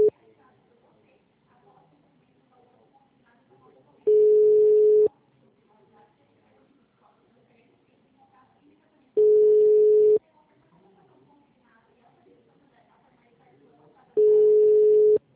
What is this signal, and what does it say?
Ringback tone (Europe, 425 Hz), with speech.
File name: out- -2013- - - .67025